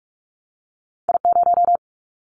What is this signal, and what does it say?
Morse code, heard as a keyed tone, about 46 wpm, 700 Hz. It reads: I0
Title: Callsign Prefixes for Top 25 DXCC - 45wpm